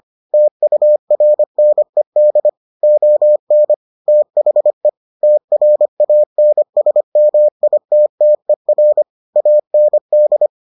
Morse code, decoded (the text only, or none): TURNED ON THE TRANSMITTER AND